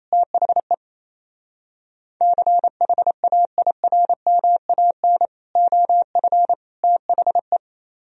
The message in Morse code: THE CHAIRMAN OF THE

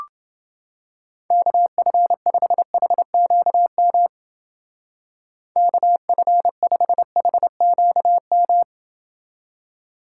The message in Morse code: KF5HQM KF5HQM

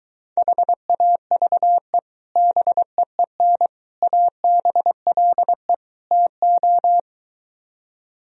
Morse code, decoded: HAVE BEEN ABLE TO

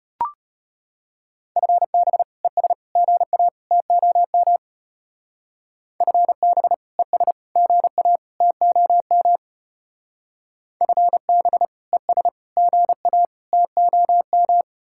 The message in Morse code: FB ES GA TOM FB ES GA TOM FB ES GA TOM

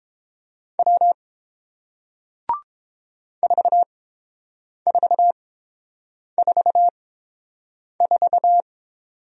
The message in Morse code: W 4444

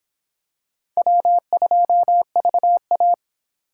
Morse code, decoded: W2VA

